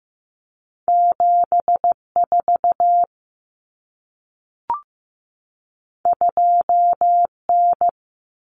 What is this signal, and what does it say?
Morse code, heard as a keyed tone, about 15 wpm, 700 Hz. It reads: 7 4 2 N
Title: Common Cut Numbers - 2-Digit 15wpm